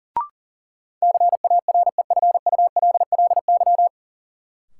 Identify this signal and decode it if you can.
Morse code, heard as a keyed tone, about 37 wpm, 700 Hz. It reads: CAREFULLY
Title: Sentences from Top 1300 Words - New Word Drill Alt - 40wpm